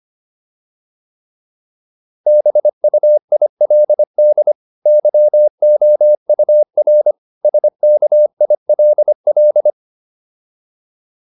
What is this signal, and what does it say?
Morse code, heard as a keyed tone, about 25 wpm, 600 Hz. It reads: BUILD YOUR SKILL